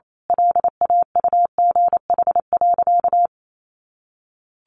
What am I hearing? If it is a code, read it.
Morse code, heard as a keyed tone, about 28 wpm, 700 Hz. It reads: LAUGH.
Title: Sentences from Top 600 Words 28wpm